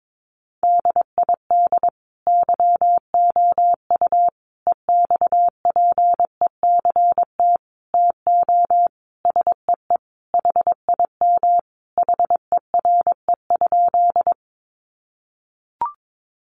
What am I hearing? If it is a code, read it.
Morse code, heard as a keyed tone, about 22 wpm, 700 Hz. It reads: DID YOU EXPECT TO SEE HIM HERE?
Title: Sentences from Top 1100 Words 22wpm